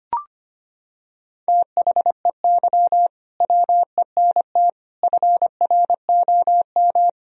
Morse code, heard as a keyed tone, about 25 wpm, 700 Hz. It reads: THEY WENT FROM
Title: Sentences from Top 1000 Words - Review Part 2 - 25wpm